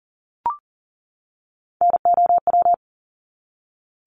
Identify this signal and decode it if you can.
Morse code, heard as a keyed tone, about 39 wpm, 700 Hz. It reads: NOW
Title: Top 200 Words 40wpm